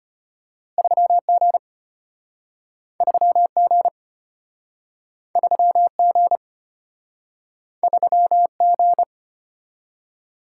Morse code, heard as a keyed tone, about 34 wpm, 700 Hz. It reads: 3G 3G 3G 3G